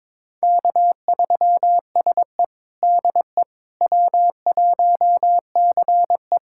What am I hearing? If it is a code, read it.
Morse code, heard as a keyed tone, about 22 wpm, 700 Hz. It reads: K3SE DE W1CE
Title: USA Callsign DE Callsign - 4 Characters - 22wpm